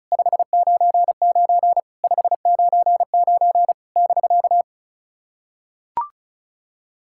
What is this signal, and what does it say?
Morse code, heard as a keyed tone, about 35 wpm, 700 Hz. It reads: 599 599 <BK>
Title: SOTA QSO Practice 35wpm